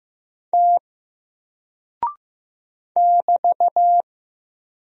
Morse code, heard as a keyed tone, about 15 wpm, 700 Hz. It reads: T <BT>
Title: Single Character (Letter:Number:Punctuation:Prosign) 15wpm